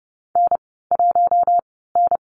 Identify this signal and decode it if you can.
Morse code, heard as a keyed tone, about 30 wpm, 700 Hz. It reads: N 1 N
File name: Lesson 37g - Letter x - Call Signs 30wpm